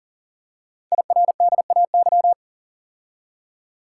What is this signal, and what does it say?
Morse code, heard as a keyed tone, about 41 wpm, 700 Hz. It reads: IRDAY
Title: Code Groups - 5 Letters - 40wpm